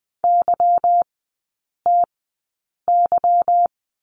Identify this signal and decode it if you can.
Morse code, heard as a keyed tone, about 20 wpm, spaced as for 8 wpm, 700 Hz. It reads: YTY
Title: 145 Lesson 24e - Letter y - Three Characters 20wpm